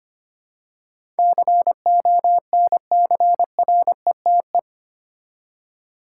Morse code, heard as a keyed tone, about 25 wpm, 690 Hz. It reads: CONCRETE
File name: 8 Letter Words - First 3 Letters Given - 25wpm